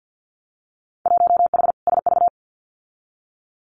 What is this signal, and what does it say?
Morse code, heard as a keyed tone, about 50 wpm, 700 Hz. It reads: 1H SV
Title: Winter Field Day Exchanges 50wpm